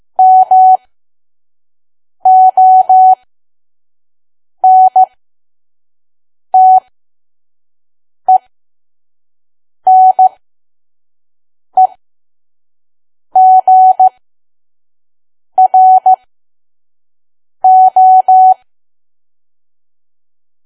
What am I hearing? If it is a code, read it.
Morse code, heard as a keyed tone, about 15 wpm, spaced as for 5 wpm, 750 Hz. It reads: MONTENEGRO
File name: Archive / 5 WPM files / PM